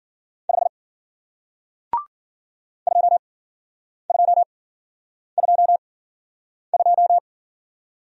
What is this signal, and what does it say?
Morse code, heard as a keyed tone, about 52 wpm, spaced as for 9 wpm, 700 Hz. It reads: 5 2222